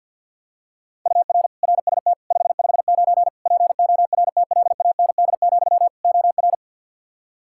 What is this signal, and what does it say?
Morse code, heard as a keyed tone, about 52 wpm, 700 Hz. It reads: UR RST 559 PORTLAND, OR
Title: K1USN SST QSO Practice 50wpm